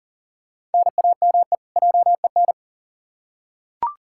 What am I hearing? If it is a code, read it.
Morse code, heard as a keyed tone, about 40 wpm, 700 Hz. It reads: NAME JEN
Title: K1USN SST QSO Practice 40wpm